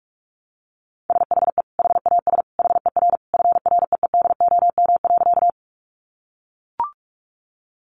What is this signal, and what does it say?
Morse code, heard as a keyed tone, about 44 wpm, 700 Hz. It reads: SHE HAS HER FREEDOM.
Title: Sentences from Top 800 Words 45wpm